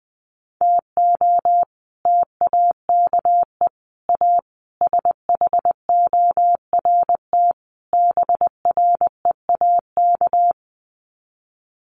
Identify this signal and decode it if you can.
Morse code, heard as a keyed tone, about 19 wpm, 700 Hz. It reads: TO TAKE A SHORT BREAK
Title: Sets of 5 Words Encore2 20wpm